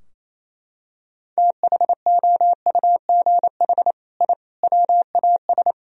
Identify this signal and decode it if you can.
Morse code, heard as a keyed tone, about 28 wpm, 700 Hz. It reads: THOUGH I WAS
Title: Sentences from Top 1000 Words - Review Part 4 - 28wpm